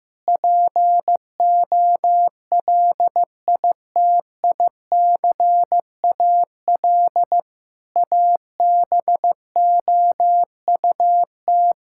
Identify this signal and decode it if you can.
Morse code, heard as a keyed tone, about 16 wpm, 700 Hz. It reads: POLITICAL ABOUT
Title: Sentences from Top 1000 Words - Review Part 5 - 15wpm